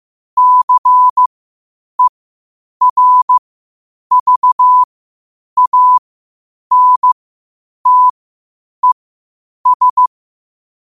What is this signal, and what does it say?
Morse code, heard as a keyed tone, about 15 wpm, spaced as for 8 wpm, 1000 Hz. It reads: CERVANTES
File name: qotdfiles / 8wpm